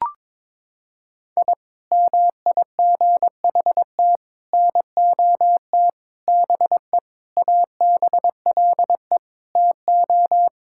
Morse code, heard as a keyed tone, about 21 wpm, 700 Hz. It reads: I MIGHT NOT BE ABLE TO
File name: Sentences with 4-Word Phrases - 22wpm